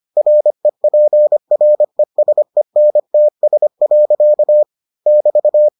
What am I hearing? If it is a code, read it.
Morse code, heard as a keyed tone, about 25 wpm, 600 Hz. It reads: REPRESENTS. <BT>